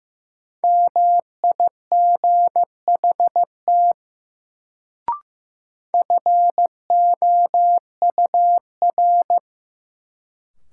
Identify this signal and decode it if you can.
Morse code, heard as a keyed tone, about 15 wpm, 700 Hz. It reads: MIGHT FOUR
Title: Sentences from Top 300 Words - New Word Drill - 15wpm